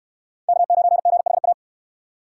Morse code, heard as a keyed tone, about 66 wpm, 700 Hz. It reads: K9GHA